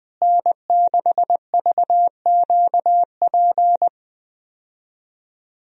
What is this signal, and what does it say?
Morse code, heard as a keyed tone, about 20 wpm, 700 Hz. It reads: N6VQP
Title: USA Call Signs - 5 Characters - Repeated 3x - 20wpm